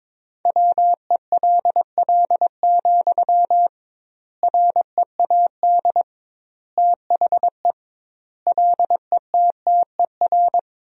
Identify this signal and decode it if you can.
Morse code, heard as a keyed tone, about 22 wpm, 700 Hz. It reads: WELL, READ THE LETTER
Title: 2x Word-Spacing - Sentences from Top 300 Words - 22wpm